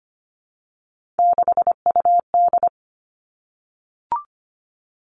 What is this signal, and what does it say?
Morse code, heard as a keyed tone, about 25 wpm, 700 Hz. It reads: 6UD
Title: Three Challenging Letter-Number Combinations 25wpm